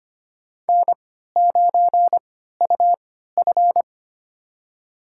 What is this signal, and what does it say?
Morse code, heard as a keyed tone, about 25 wpm, spaced as for 14 wpm, 700 Hz. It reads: N9UF
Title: USA Call Signs - 3x Character Spacing - 25wpm